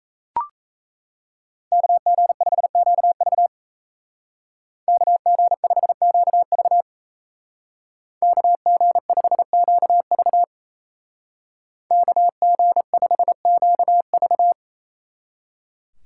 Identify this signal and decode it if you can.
Morse code, heard as a keyed tone, about 34 wpm, 700 Hz. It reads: KG5QV KG5QV KG5QV KG5QV
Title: Speed Racing - USA 2x2 Call Signs 28wpm